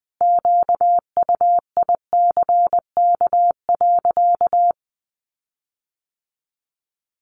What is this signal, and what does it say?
Morse code, heard as a keyed tone, about 19 wpm, 700 Hz. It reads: QUICK.
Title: Sentences from Top 1300 Words 20wpm